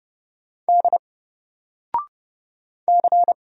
Morse code, heard as a keyed tone, about 30 wpm, 700 Hz. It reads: D C
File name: Single-Letter-Number-30wpm